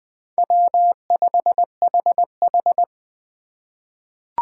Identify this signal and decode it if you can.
Morse code, heard as a keyed tone, about 20 wpm, 700 Hz. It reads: W5HH
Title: USA Call Signs - 4 Characters - 6x Character Spacing - 20wpm